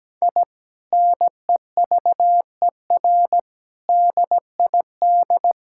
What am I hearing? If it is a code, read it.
Morse code, heard as a keyed tone, about 17 wpm, 700 Hz. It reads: I NEVER DID
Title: Sentences from Top 1000 Words - Review Part 5 - 17wpm